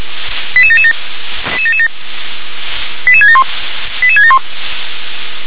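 Five-tone selective call (ZVEI), heard as quick runs of five tones, tones of about 70 ms. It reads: 88777 88671 88671
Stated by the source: ZVEI five-tone selective call for code 88679, received by SDR.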